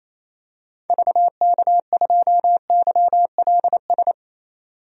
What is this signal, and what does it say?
Morse code, heard as a keyed tone, about 28 wpm, 700 Hz. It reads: VK2YLS
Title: DXCC-100 Call Signs - 6 Characters - 28wpm